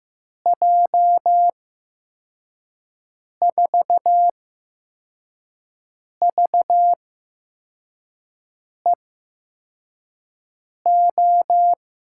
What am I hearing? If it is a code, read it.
Morse code, heard as a keyed tone, about 15 wpm, spaced as for 4 wpm, 700 Hz. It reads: J4VEO
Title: USA Call Signs - 8x Character Spacing - 15wpm